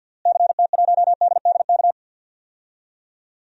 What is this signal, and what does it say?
Morse code, heard as a keyed tone, about 52 wpm, 700 Hz. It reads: CT1DDK
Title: DXCC-100 Call Signs - 6 Characters - Repeated 3x - 50wpm